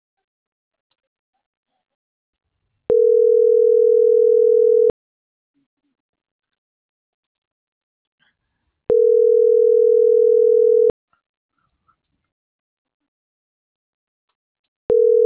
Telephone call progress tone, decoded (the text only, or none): ringback tone